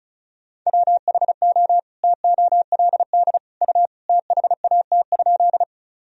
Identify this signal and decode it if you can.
Morse code, heard as a keyed tone, about 36 wpm, 700 Hz. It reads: WHO TOLD U THAT?
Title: Sentences from Top 400 Words and Abbreviations 35wpm